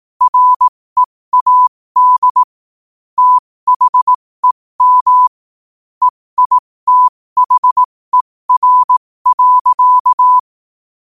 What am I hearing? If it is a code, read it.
Morse code, heard as a keyed tone, about 18 wpm, spaced as for 15 wpm, 1000 Hz. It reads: READ THEM EITHER.
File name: qotdfiles / 15wpm